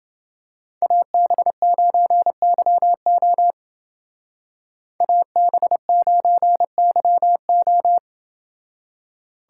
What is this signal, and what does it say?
Morse code, heard as a keyed tone, about 29 wpm, 700 Hz. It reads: AB9YO AB9YO